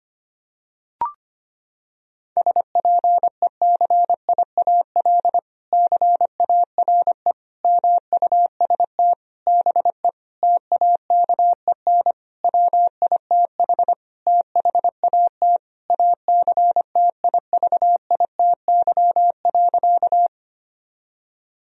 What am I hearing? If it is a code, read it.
Morse code, heard as a keyed tone, about 25 wpm, 700 Hz. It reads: SPECIAL CARE MUST BE TAKEN WITH THAT ACTIVITY.
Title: Sentences from Top 1000 Words - Review Part 4 - 25wpm